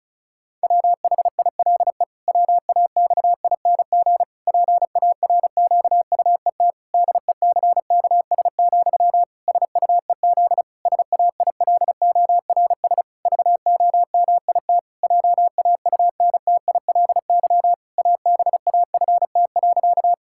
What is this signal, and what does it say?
Morse code, heard as a keyed tone, about 36 wpm, 700 Hz. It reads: WHILE WAXING PARQUET DECKS, SUEZ SAILORS VOMIT JAUNTILY ABAFT.
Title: Pangram Challenge v2 - 35wpm